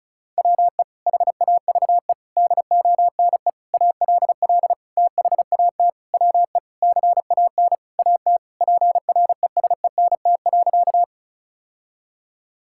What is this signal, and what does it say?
Morse code, heard as a keyed tone, about 34 wpm, 700 Hz. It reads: WE HAVE DONE ALL THAT WE CAN AT PRESENT.
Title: Sentences from Top 1000 Words - Review Part 5 - 35wpm